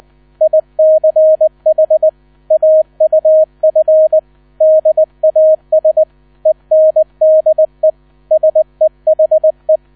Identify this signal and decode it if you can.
Morse code, heard as a keyed tone, about 20 wpm, 630 Hz, with radio noise. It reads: ICH AUF DAS ENDE SEHE
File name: 000 Vorwort 20wpm